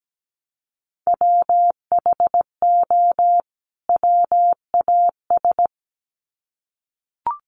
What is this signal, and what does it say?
Morse code, heard as a keyed tone, about 17 wpm, 700 Hz. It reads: WHO WAS
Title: Sets of 2 Words 17wpm